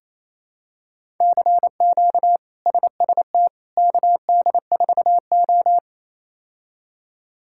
Morse code, heard as a keyed tone, about 28 wpm, 700 Hz. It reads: CQ SST KD4O
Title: K1USN SST QSO Practice - 28wpm